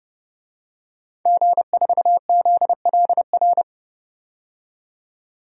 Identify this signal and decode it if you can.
Morse code, heard as a keyed tone, about 30 wpm, 700 Hz. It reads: G4ZLR
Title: DXCC-100 Call Signs - 5 Characters - 30wpm